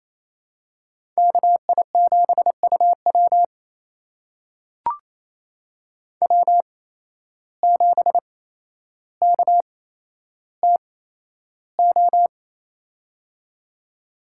Morse code, heard as a keyed tone, about 28 wpm, spaced as for 12 wpm, 700 Hz. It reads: KI7UW W7KTO
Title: USA Call Signs - 5 Characters - 8x Character Spacing - 28wpm